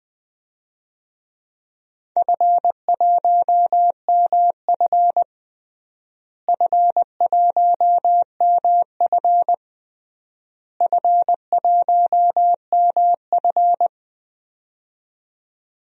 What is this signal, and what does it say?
Morse code, heard as a keyed tone, about 21 wpm, 700 Hz. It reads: F1MF F1MF F1MF
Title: DXCC-100 Call Signs - 4 Characters - Repeated 3x - 20wpm